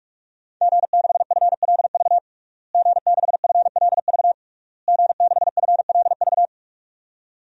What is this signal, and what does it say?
Morse code, heard as a keyed tone, about 46 wpm, 700 Hz. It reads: G6FLV G6FLV G6FLV
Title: DXCC-100 Call Signs - 5 Characters - Repeated 3x - 45wpm